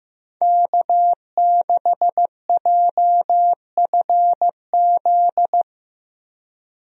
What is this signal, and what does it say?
Morse code, heard as a keyed tone, about 15 wpm, 700 Hz. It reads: K6JFZ
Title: USA Call Signs - 5 Characters - Repeated 2x - 15wpm